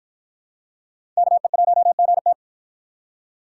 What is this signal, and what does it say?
Morse code, heard as a keyed tone, about 53 wpm, 700 Hz. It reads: KE1GT